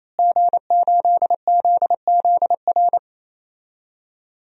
Morse code, heard as a keyed tone, about 28 wpm, 700 Hz. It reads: G8ZZR
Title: DXCC-100 Call Signs - 5 Characters - Repeated 3x - 28wpm